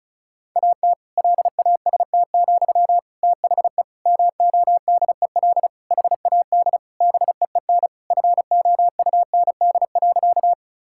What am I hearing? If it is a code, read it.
Morse code, heard as a keyed tone, about 36 wpm, 700 Hz. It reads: AT LAST, THE MODEL HAD BEEN FOUND.
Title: Sentences from Top 1000 Words - Review Part 5 - 35wpm